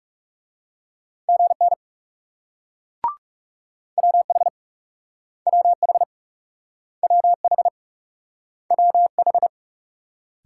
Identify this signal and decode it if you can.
Morse code, heard as a keyed tone, about 40 wpm, 700 Hz. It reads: GN WH WH WH WH